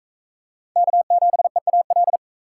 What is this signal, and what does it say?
Morse code, heard as a keyed tone, about 44 wpm, 700 Hz. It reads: K7EAL